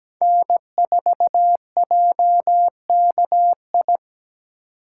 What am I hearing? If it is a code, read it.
Morse code, heard as a keyed tone, about 17 wpm, 700 Hz. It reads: N4JKI
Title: USA Call Signs - 6x Character Spacing - 17wpm